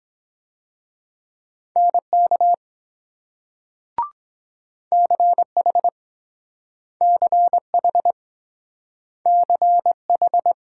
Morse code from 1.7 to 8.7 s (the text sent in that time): NK CH CH